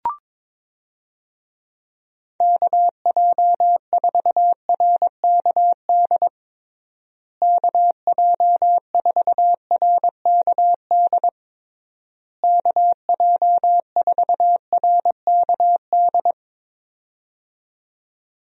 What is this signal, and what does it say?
Morse code, heard as a keyed tone, about 22 wpm, 700 Hz. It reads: KJ4RKD KJ4RKD KJ4RKD
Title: USA Call Signs - 6 Characters - Repeated 3x - 22wpm